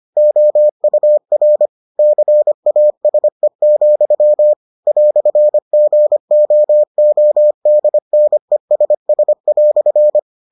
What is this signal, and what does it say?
Morse code, heard as a keyed tone, about 25 wpm, 600 Hz. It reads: OUR CASE, "GOODNESS"